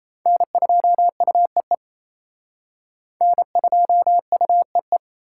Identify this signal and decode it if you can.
Morse code, heard as a keyed tone, about 34 wpm, 700 Hz. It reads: N2UEE N2UEE
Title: Speed Racing - USA 1x3 Call Signs 28wpm